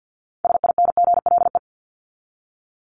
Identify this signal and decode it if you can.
Morse code, heard as a keyed tone, about 50 wpm, 700 Hz. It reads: SINGLE
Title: Top 501-750 Words 50wpm